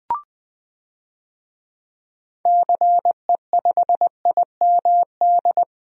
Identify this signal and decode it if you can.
Morse code, heard as a keyed tone, about 20 wpm, 700 Hz. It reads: CE5IMD
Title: DXCC-100 Call Signs - 6 Characters - Repeated 2x - 20wpm